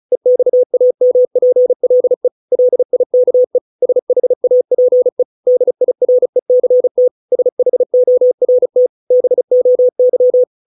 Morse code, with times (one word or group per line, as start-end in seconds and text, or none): 0.1-2.3 EXAMPLE
2.5-3.6 LIKE
3.8-5.2 SHAPE
5.5-7.1 DIRECT
7.3-8.9 SHORT
9.1-10.4 BOY